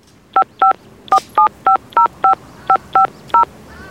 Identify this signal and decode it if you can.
DTMF, heard as telephone touch tones, tones of about 110 ms. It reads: 5547505550